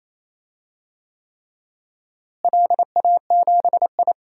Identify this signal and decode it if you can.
Morse code, heard as a keyed tone, about 28 wpm, 700 Hz. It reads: LA7I
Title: DXCC-100 Call Signs - 4 Characters - Repeated 3x - 28wpm